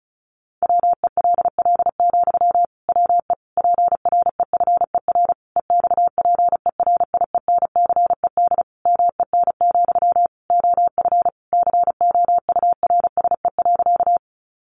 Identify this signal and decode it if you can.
Morse code, heard as a keyed tone, about 34 wpm, 700 Hz. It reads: WELL, WE PREFER EXPERIENCED MEN, OF COURSE.
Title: Sentences from Top 1100 Words 35wpm